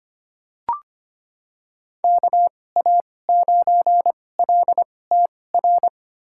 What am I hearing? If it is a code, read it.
Morse code, heard as a keyed tone, about 25 wpm, spaced as for 18 wpm, 700 Hz. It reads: KA9LTR
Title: USA Call Signs - 2x Character Spacing - 25wpm